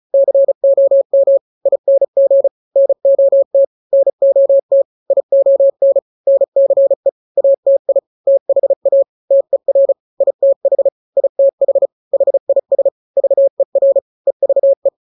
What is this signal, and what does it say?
Morse code, heard as a keyed tone, about 34 wpm, 550 Hz. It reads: COM ING NOT NOT ION NCE ATI THA TER ITH ITH HIS VER EVE
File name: 3-grams 35 30 WPM